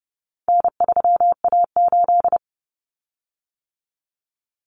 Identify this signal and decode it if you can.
Morse code, heard as a keyed tone, about 30 wpm, 700 Hz. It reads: N3A8
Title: Common Cut Numbers - 4-Digit 30wpm